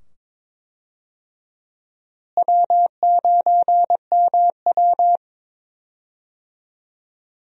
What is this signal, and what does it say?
Morse code, heard as a keyed tone, about 22 wpm, 700 Hz. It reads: W9MW
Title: USA Call Signs - 4 Characters - Repeated 2x - 22wpm